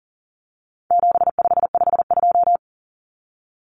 Morse code, heard as a keyed tone, about 40 wpm, 700 Hz. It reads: 7552
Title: Common Cut Numbers - 4-Digit 40wpm